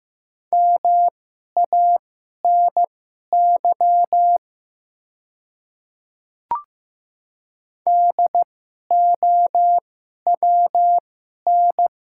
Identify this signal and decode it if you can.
Morse code, heard as a keyed tone, about 15 wpm, spaced as for 11 wpm, 700 Hz. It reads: MANY DOWN
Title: 2x Character Spacing - Top 100 Words - 15wpm